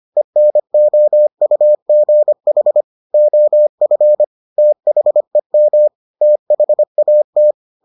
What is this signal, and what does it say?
Morse code, heard as a keyed tone, about 25 wpm, 600 Hz. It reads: ENOUGH OF THEM THAT